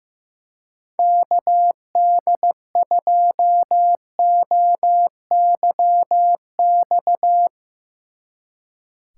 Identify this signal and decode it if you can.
Morse code, heard as a keyed tone, about 16 wpm, 700 Hz. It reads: KD2OYX
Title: Speed Racing - USA Call Signs 15wpm